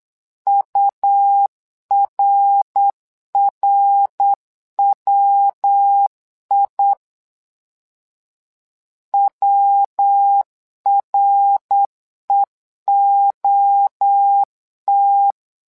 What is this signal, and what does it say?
Morse code, heard as a keyed tone, about 8 wpm, 800 Hz. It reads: URRWI WREOT
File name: LEZIONI / 3 MWRO / 3 MWRO 8wpm 2606